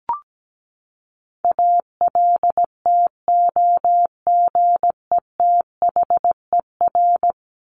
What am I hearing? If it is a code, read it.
Morse code, heard as a keyed tone, about 17 wpm, 700 Hz. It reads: ALTOGETHER
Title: Top 1501-1750 Words 17wpm